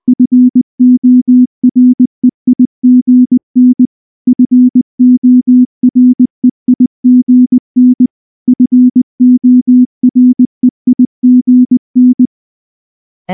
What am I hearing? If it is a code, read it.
Morse code, heard as a keyed tone, about 19 wpm, 250 Hz. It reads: FOREIGN FOREIGN FOREIGN
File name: morse / morse-training-20WPM-7